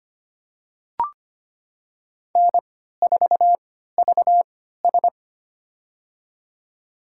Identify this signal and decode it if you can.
Morse code, heard as a keyed tone, about 25 wpm, spaced as for 14 wpm, 700 Hz. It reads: N4VS
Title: USA Call Signs - 4 Characters - 3x Character Spacing - 25wpm